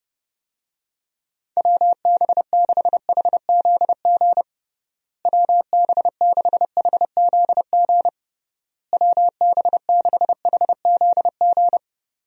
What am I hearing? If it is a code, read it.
Morse code, heard as a keyed tone, about 31 wpm, 700 Hz. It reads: WB6HZG WB6HZG WB6HZG
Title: USA Call Signs - 6 Characters - Repeated 3x - 30wpm